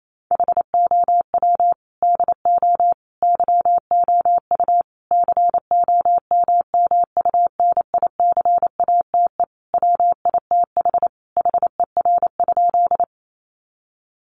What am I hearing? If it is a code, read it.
Morse code, heard as a keyed tone, about 28 wpm, 700 Hz. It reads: HOW DO YOU COMMUNICATE WITH HER?
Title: Sentences from Top 1300 Words 28wpm